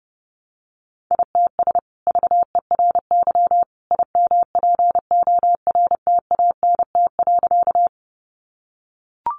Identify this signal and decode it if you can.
Morse code, heard as a keyed tone, about 28 wpm, 700 Hz. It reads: ITS VERY IMPORTANT.
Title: Sentences from Top 600 Words 30wpm